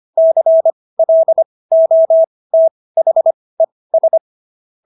Morse code, heard as a keyed tone, about 25 wpm, spaced as for 18 wpm, 650 Hz. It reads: CLOTHES